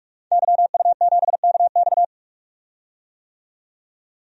Morse code, heard as a keyed tone, about 46 wpm, 700 Hz. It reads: YU7KX
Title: DXCC-100 Call Signs - 5 Characters - 45wpm